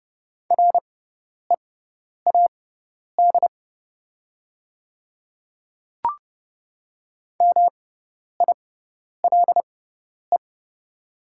Morse code, heard as a keyed tone, about 29 wpm, spaced as for 10 wpm, 700 Hz. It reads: READ MILE
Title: 6x Character Spacing - Top 300 Words - 30wpm